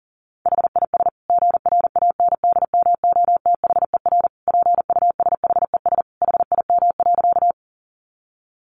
Morse code, heard as a keyed tone, about 40 wpm, 700 Hz. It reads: HIS GRANDMOTHER PUSHES HIM.
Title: Additional Easy Phrases 40wpm